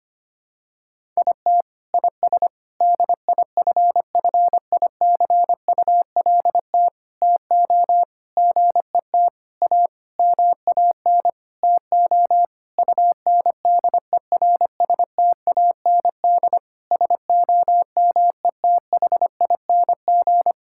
Morse code, heard as a keyed tone, about 25 wpm, 700 Hz. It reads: IT IS DIFFICULT TO GET A MAN TO UNDERSTAND SOMETHING
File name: practice / 25wpm / Sentences with 9-Word Phrases - 25wpm